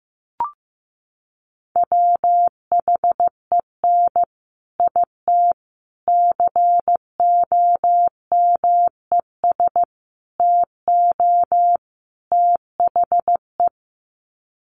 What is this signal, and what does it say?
Morse code, heard as a keyed tone, about 15 wpm, 700 Hz. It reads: WHEN IT COMES TO THE
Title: Sets of 5 Words 15wpm